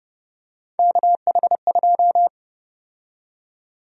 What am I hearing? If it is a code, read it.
Morse code, heard as a keyed tone, about 30 wpm, 700 Hz. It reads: KH2
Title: USA Call Sign Prefixes - 30wpm